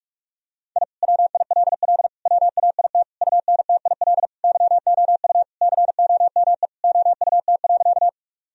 Morse code, heard as a keyed tone, about 41 wpm, 700 Hz. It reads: I WILL WAIT UNTIL YOU COME OUT.
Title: Sentences from Top 1000 Words - Review Part 1 - 45wpm